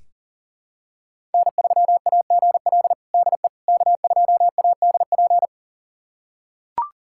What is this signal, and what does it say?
Morse code, heard as a keyed tone, about 40 wpm, 700 Hz. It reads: N3AGL DE K2ADP
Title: USA Callsign DE Callsign - 5 Characters - 40wpm